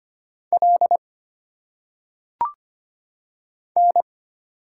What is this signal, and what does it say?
Morse code, heard as a keyed tone, about 25 wpm, 700 Hz. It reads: L N
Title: Single Letters - Phonetic Pronunciation 25wpm